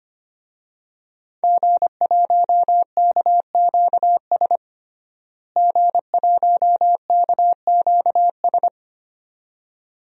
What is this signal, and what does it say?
Morse code, heard as a keyed tone, about 25 wpm, 700 Hz. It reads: G1KQS G1KQS
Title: DXCC-100 Call Signs - 5 Characters - Repeated 2x - 25wpm